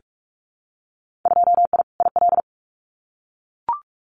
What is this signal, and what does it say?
Morse code, heard as a keyed tone, about 44 wpm, 700 Hz. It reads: 2I IL
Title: Winter Field Day Exchanges 45wpm